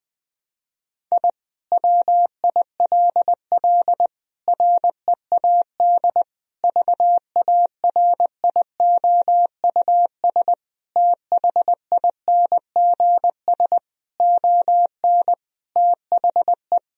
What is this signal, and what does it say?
Morse code, heard as a keyed tone, about 20 wpm, 700 Hz. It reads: I WILL READ VARIOUS THINGS ON THE
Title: Sentences from Top 1000 Words - Review Part 4 - 20wpm